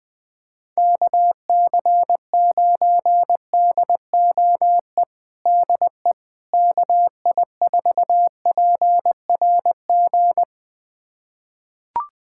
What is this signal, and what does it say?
Morse code, heard as a keyed tone, about 20 wpm, 700 Hz. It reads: KC9DOE DE KI4PRG
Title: USA Callsign DE Callsign - 6 Characters - 20wpm